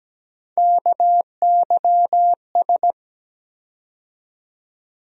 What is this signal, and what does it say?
Morse code, heard as a keyed tone, about 17 wpm, 700 Hz. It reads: KYS
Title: USA Call Sign Postfixes - 3 Letters 17wpm